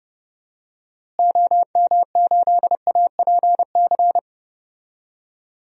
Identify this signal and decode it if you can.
Morse code, heard as a keyed tone, about 32 wpm, 700 Hz. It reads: OM8APC
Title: DXCC-100 Call Signs - 6 Characters - Repeated 3x - 30wpm